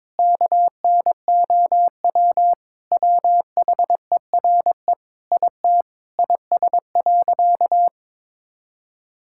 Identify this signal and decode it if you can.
Morse code, heard as a keyed tone, about 22 wpm, 700 Hz. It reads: KNOW WHERE IT IS.
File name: practice / 22wpm / Sentences with 4-Word Phrases - 22wpm